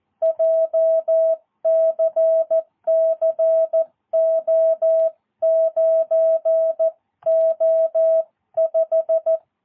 Morse code, heard as a keyed tone, about 14 wpm, 650 Hz. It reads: JCC O9O5
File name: jh8cbh / 2022cw